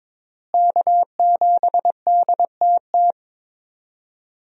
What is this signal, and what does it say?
Morse code, heard as a keyed tone, about 22 wpm, 700 Hz. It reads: K7DTT
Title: USA Call Signs - 5 Characters - Repeated 3x - 22wpm